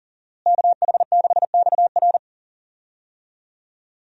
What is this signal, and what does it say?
Morse code, heard as a keyed tone, about 40 wpm, 700 Hz. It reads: KH6XR
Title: DXCC-100 Call Signs - 5 Characters - Repeated 3x - 40wpm